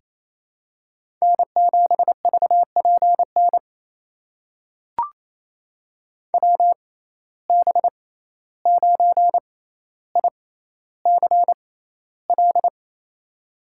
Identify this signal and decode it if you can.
Morse code, heard as a keyed tone, about 28 wpm, spaced as for 14 wpm, 700 Hz. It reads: N7VPN WB9ICL
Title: USA Call Signs - 6x Character Spacing - 28wpm